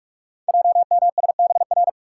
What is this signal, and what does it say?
Morse code, heard as a keyed tone, about 44 wpm, 690 Hz. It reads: JMSBR
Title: Code Groups - 5 Letters - 45wpm